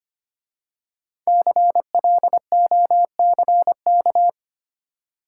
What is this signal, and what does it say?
Morse code, heard as a keyed tone, about 25 wpm, 700 Hz. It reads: CLOCK
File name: 5 Letter Words - First 3 Letters Given - 25wpm